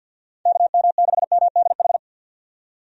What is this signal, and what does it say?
Morse code, heard as a keyed tone, about 50 wpm, 700 Hz. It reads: KM6MDH
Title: USA Call Signs - 6 Characters - 50wpm